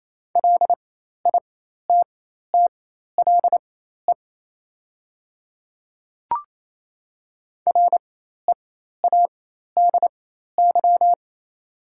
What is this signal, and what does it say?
Morse code, heard as a keyed tone, about 28 wpm, spaced as for 13 wpm, 700 Hz. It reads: LITTLE READY
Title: 4x Character Spacing - Top 500 Words - 28wpm